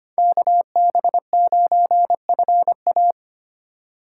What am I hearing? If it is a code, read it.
Morse code, heard as a keyed tone, about 32 wpm, 700 Hz. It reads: KB9FA